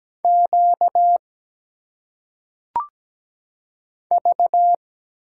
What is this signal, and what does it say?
Morse code, heard as a keyed tone, about 17 wpm, 700 Hz. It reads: Q V
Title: Single Letters - Phonetic Pronunciation 17wpm